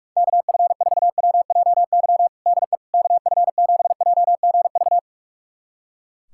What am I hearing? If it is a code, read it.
Morse code, heard as a keyed tone, about 46 wpm, 700 Hz. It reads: KF4WJY DE KF7JGV
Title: USA Callsign DE Callsign 45wpm